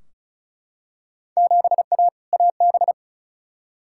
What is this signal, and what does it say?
Morse code, heard as a keyed tone, about 36 wpm, 700 Hz. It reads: 7A AB
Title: Field Day Exchanges Encore - 35wpm